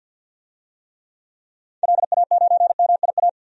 Morse code, heard as a keyed tone, about 52 wpm, 700 Hz. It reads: LA9MIA
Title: DXCC-100 Call Signs - 6 Characters - 50wpm